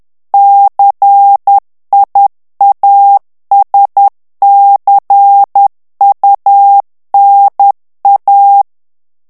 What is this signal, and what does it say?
Morse code, heard as a keyed tone, about 10 wpm, 800 Hz. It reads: CIASCUNA
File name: ITALIEN texte clair 2 vitesse 10 mots minute